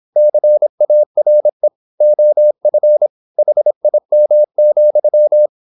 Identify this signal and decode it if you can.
Morse code, heard as a keyed tone, about 26 wpm, 600 Hz. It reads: CARE OF HIM,
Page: Never 26wpm | CW med Gnister